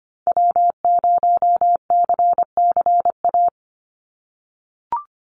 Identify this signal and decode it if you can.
Morse code, heard as a keyed tone, about 25 wpm, 700 Hz. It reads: W0CCA
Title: NA SOTA Activator Callsigns 25wpm